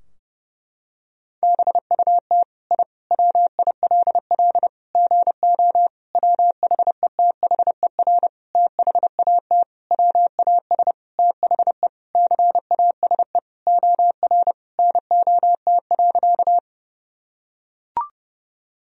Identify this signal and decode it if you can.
Morse code, heard as a keyed tone, about 30 wpm, 700 Hz. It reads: BUT I WILL GO WHETHER THAT WAS THE CASE OR NOT.
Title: Sentences from Top 1000 Words - Review Part 3 - 30wpm